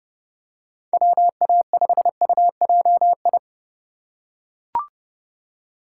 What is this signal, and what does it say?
Morse code, heard as a keyed tone, about 30 wpm, 700 Hz. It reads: WA5UJI
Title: USA Call Signs - 6 Characters - 2x Character Spacing - 30wpm